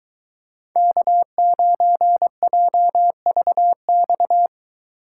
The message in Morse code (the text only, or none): K9JVX